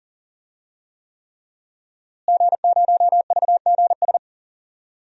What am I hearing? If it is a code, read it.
Morse code, heard as a keyed tone, about 39 wpm, 700 Hz. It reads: G0VGS
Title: DXCC-100 Call Signs - 5 Characters - Repeated 3x - 40wpm